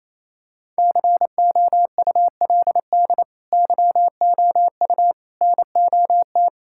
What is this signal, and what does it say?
Morse code, heard as a keyed tone, about 28 wpm, 700 Hz. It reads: COULD YOU NOT
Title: Sentences from Top 1600 Words 28wpm